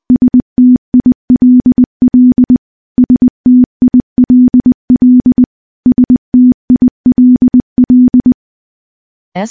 Morse code, heard as a keyed tone, about 20 wpm, 250 Hz. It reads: STILL STILL STILL
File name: morse / morse-training-20WPM